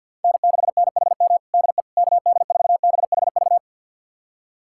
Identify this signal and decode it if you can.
Morse code, heard as a keyed tone, about 50 wpm, 700 Hz. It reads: N6NHM DE KD4BHV
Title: USA Callsign DE Callsign 50wpm